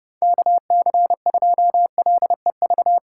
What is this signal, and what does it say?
Morse code, heard as a keyed tone, about 32 wpm, 700 Hz. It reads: KC2LEV
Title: USA Call Signs - 4x Character Spacing - 30wpm